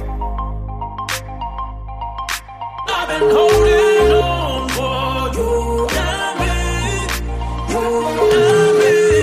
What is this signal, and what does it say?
Ringback tone (Europe, 425 Hz), with noise.